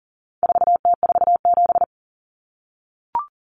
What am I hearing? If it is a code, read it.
Morse code, heard as a keyed tone, about 40 wpm, 700 Hz. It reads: 4T47
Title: Common Cut Numbers - 4-Digit 40wpm